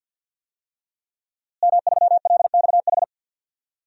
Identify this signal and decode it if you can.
Morse code, heard as a keyed tone, about 52 wpm, 700 Hz. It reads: M3LXH
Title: DXCC-100 Call Signs - 5 Characters - Repeated 2x - 50wpm